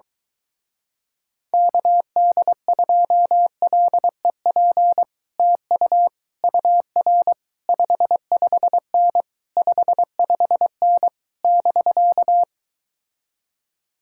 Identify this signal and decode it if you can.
Morse code, heard as a keyed tone, about 23 wpm, 700 Hz. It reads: KD2LEP TU UR 55N 55N <BK>